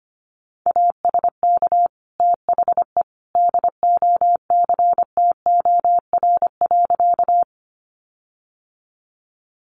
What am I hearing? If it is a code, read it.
Morse code, heard as a keyed tone, about 25 wpm, 700 Hz. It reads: ASK THE DOCTOR.
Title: Sentences from Top 1500 Words 25wpm